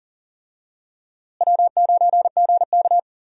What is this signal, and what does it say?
Morse code, heard as a keyed tone, about 40 wpm, 700 Hz. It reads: W9GK
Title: USA Call Signs - 4 Characters - Repeated 3x - 40wpm